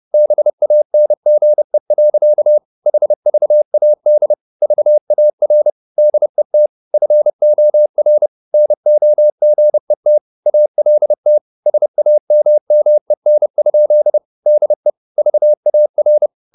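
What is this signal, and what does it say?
Morse code, heard as a keyed tone, about 30 wpm, 600 Hz. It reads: BANGE. HVAD VAR DET FOR NOGET ALT SAMMEN? DE VAR
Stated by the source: High-speed Morse (QRQ) at 30 wpm.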